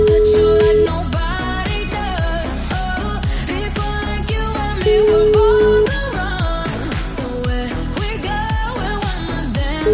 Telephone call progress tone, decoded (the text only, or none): ringback tone